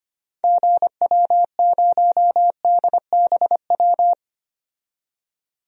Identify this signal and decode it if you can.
Morse code, heard as a keyed tone, about 25 wpm, 690 Hz. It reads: GW0DBW
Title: DXCC-100 Call Signs - 6 Characters - 25wpm